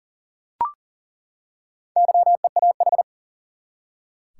Morse code, heard as a keyed tone, about 40 wpm, 700 Hz. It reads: YEAH
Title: Four Letter Words - 40wpm